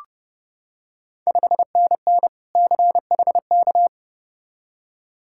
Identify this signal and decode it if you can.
Morse code, heard as a keyed tone, about 30 wpm, 700 Hz. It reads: 5NN CHK
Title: Ontario QSO Party Exchange 30wpm